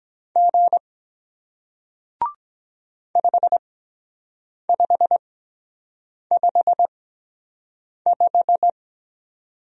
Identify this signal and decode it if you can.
Morse code, heard as a keyed tone, about 24 wpm, spaced as for 7 wpm, 700 Hz. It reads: G 5555